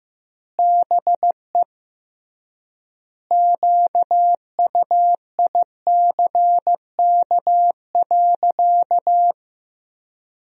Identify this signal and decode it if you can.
Morse code, heard as a keyed tone, about 15 wpm, 700 Hz. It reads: BE QUICK.
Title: 3x Word-Spacing - Sentences from Top 500 Words - 15wpm